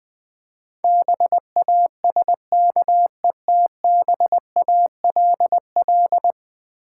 Morse code, heard as a keyed tone, about 21 wpm, 700 Hz. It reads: BASKETBALL
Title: Difficult Words with Hints - 20wpm